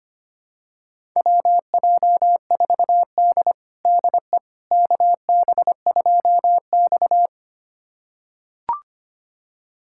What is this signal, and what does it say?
Morse code, heard as a keyed tone, about 25 wpm, 700 Hz. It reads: WJ4D DE KB2X
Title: USA Callsign DE Callsign - 4 Characters - 25wpm